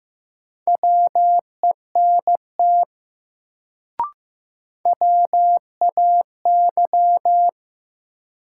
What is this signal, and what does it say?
Morse code, heard as a keyed tone, about 15 wpm, 700 Hz. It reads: WENT WAY
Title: Sets of 2 Words - New Word Drill - 15wpm